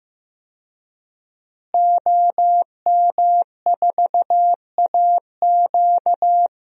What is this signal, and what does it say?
Morse code, heard as a keyed tone, about 16 wpm, 700 Hz. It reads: OM4AQ
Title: DXCC-100 Call Signs - 5 Characters - Repeated 3x - 15wpm